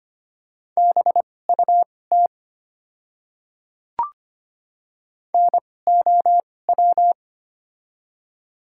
Morse code, heard as a keyed tone, about 25 wpm, spaced as for 18 wpm, 700 Hz. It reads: BUT NOW
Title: 2x Character Spacing - Top 200 Words - 25wpm